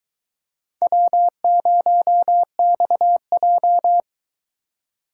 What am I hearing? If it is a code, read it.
Morse code, heard as a keyed tone, about 22 wpm, 700 Hz. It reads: W0XJ